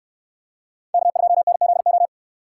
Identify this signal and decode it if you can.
Morse code, heard as a keyed tone, about 69 wpm, 700 Hz. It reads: K2NZO